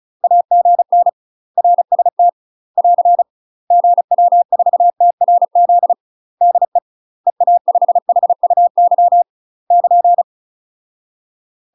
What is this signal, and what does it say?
Morse code, heard as a keyed tone, about 35 wpm, 700 Hz. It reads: AGN RST <AR> GW4TRZ DE EA5HUY <KN>
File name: UK MASTER QSOs TRAINING / 35WPM / UK MASTER QSOs 0044